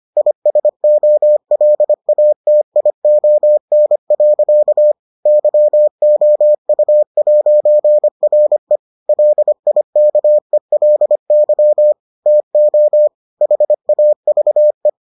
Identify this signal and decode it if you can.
Morse code, heard as a keyed tone, about 25 wpm, 600 Hz. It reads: ISOLATION. YOU'RE LIKELY TO HAVE